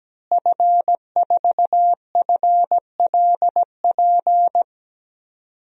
Morse code, heard as a keyed tone, about 17 wpm, 700 Hz. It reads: F4FLP
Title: DXCC-100 Call Signs - 5 Characters - Repeated 3x - 17wpm